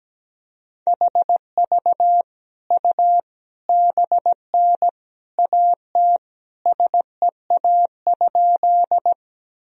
Morse code, heard as a keyed tone, about 17 wpm, 700 Hz. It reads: HV U BN AT SEA?
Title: Sentences from Top 300 Words and Abbreviations 17wpm